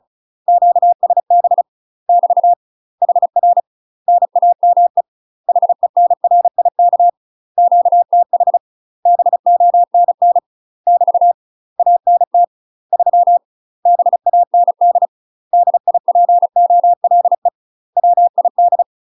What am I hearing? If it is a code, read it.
Morse code, heard as a keyed tone, about 36 wpm, 700 Hz. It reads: QSB <BT> HR NAME HENRIK QTH BONN <BT> ANT 3 BAND DIPOLE WID
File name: UK MASTER QSOs TRAINING / 35WPM / UK MASTER QSOs 0009